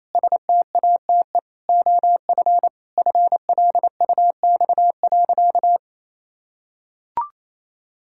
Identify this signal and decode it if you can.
Morse code, heard as a keyed tone, about 28 wpm, 700 Hz. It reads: STATE OF FLUX.
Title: Pangram Challenge v2 - 28wpm